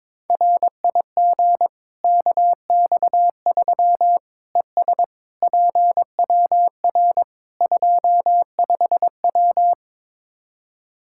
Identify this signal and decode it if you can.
Morse code, heard as a keyed tone, about 22 wpm, 700 Hz. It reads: RIG KX3 ES PWR 25W
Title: K1USN SST QSO Practice 22wpm